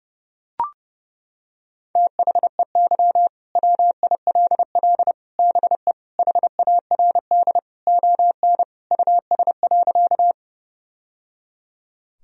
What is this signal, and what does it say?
Morse code, heard as a keyed tone, about 30 wpm, 700 Hz. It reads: THEY WILL BE HARD ON US.
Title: Sentences from Top 1000 Words - Review Part 1 - 30wpm